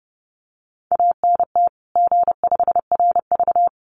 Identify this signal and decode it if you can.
Morse code, heard as a keyed tone, about 30 wpm, 700 Hz. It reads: ANT G5RV
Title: Common QSO Phrases 30wpm